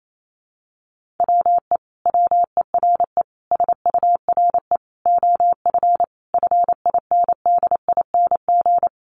Morse code, heard as a keyed tone, about 28 wpm, 700 Hz. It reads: WE WERE SURE OF FINDING
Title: Sentences from Top 1100 Words 28wpm